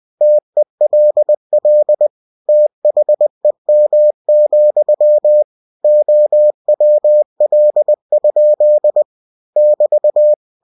Morse code, heard as a keyed tone, about 20 wpm, 600 Hz. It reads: TELL THEM, OWL? <BT>
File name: cw-books / winnie-the-pooh / winnie-20wpm-page-0102